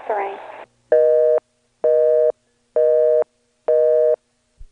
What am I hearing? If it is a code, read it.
Busy tone (North America), 480 + 620 Hz, with speech.